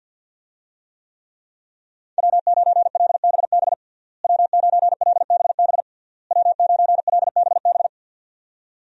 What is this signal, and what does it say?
Morse code, heard as a keyed tone, about 49 wpm, 700 Hz. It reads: W9LBB W9LBB W9LBB
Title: USA Call Signs - 5 Characters - Repeated 3x - 50wpm